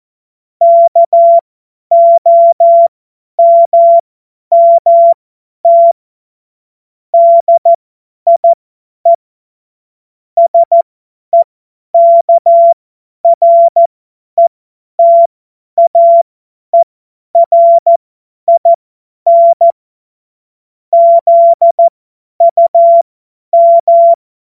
Morse code, heard as a keyed tone, about 14 wpm, spaced as for 10 wpm, 680 Hz. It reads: KOMMT DIE SEKRETAERIN ZUM
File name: Morsekurs / Witze / Witz 29 April 2025 50 F70